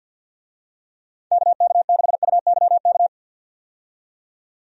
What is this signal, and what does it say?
Morse code, heard as a keyed tone, about 50 wpm, 700 Hz. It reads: KK6UYK
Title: USA Call Signs - 6 Characters - 50wpm